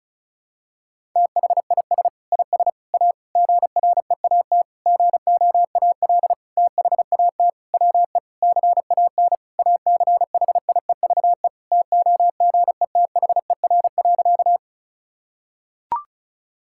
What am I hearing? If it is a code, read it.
Morse code, heard as a keyed tone, about 34 wpm, 700 Hz. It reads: THIS IS A GREAT GOAL THAT WE CAN ACHIEVE TOGETHER.
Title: Sentences from Top 1000 Words - Review Part 6 - 35wpm